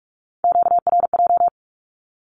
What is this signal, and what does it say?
Morse code, heard as a keyed tone, about 46 wpm, 700 Hz. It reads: QRJ
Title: Q-Codes with Spoken Meaning - 45wpm